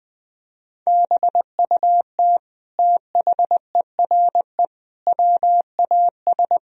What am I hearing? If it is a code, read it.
Morse code, heard as a keyed tone, about 20 wpm, 700 Hz. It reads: BUT THERE WAS
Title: Sentences from Top 1000 Words - Review Part 2 - 20wpm